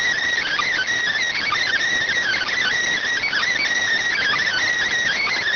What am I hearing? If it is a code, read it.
Emergency Alert System (SAME) tones (header bursts), heard as harsh buzzing bursts. It is heard from 0.4 to 4.6 s.